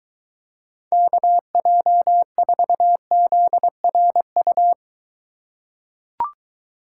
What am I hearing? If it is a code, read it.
Morse code, heard as a keyed tone, about 23 wpm, 700 Hz. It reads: KJ4ZRU